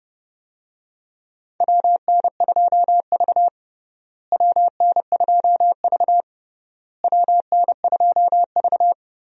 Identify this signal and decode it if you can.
Morse code, heard as a keyed tone, about 30 wpm, 700 Hz. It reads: WN2V WN2V WN2V
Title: USA Call Signs - 4 Characters - Repeated 3x - 30wpm